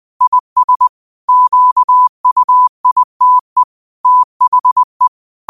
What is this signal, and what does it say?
Morse code, heard as a keyed tone, about 20 wpm, 1000 Hz. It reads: IS QUITE THE
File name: qotdfiles / 20wpm